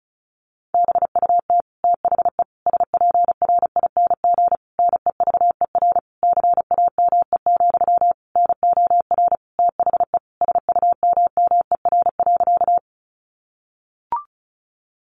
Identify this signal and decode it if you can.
Morse code, heard as a keyed tone, about 34 wpm, 700 Hz. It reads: BUT THE SPRING NEVER CAME, NOR THE SUMMER.
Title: Sentences from Top 1500 Words 35wpm